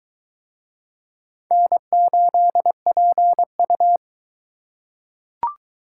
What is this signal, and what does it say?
Morse code, heard as a keyed tone, about 21 wpm, 700 Hz. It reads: N8PU